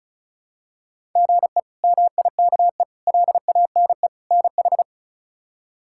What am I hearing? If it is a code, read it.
Morse code, heard as a keyed tone, about 36 wpm, 700 Hz. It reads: GE MIKE LANE NH
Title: K1USN SST QSO Practice - 35wpm